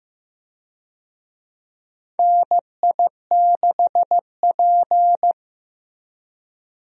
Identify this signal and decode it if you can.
Morse code, heard as a keyed tone, about 15 wpm, 700 Hz. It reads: NI6P
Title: USA Call Signs - 4 Characters - Repeated 2x - 15wpm